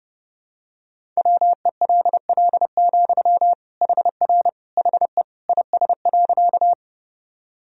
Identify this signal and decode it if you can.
Morse code, heard as a keyed tone, about 30 wpm, 700 Hz. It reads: WELL, HR HE IS.
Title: Sentences from Top 200 Words and Abbreviations 30wpm